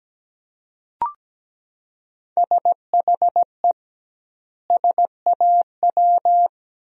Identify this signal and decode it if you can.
Morse code, heard as a keyed tone, about 17 wpm, 700 Hz. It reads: SHE SAW
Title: 2x Word-Spacing - Sentences from Top 300 Words - 17wpm